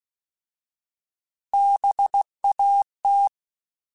Morse code, heard as a keyed tone, about 16 wpm, 800 Hz. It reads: BAT
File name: Alphabetical / Alphabetical mp3s / Bs @15